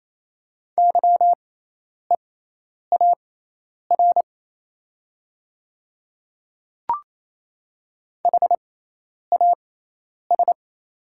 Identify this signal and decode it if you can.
Morse code, heard as a keyed tone, about 28 wpm, spaced as for 10 wpm, 700 Hz. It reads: YEAR HAS
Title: 6x Character Spacing - Top 200 Words - 28wpm